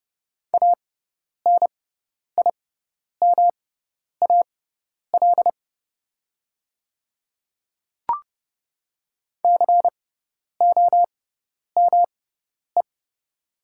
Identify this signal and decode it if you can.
Morse code, heard as a keyed tone, about 30 wpm, spaced as for 10 wpm, 690 Hz. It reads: ANIMAL COME
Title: 6x Character Spacing - Top 200 Words - 30wpm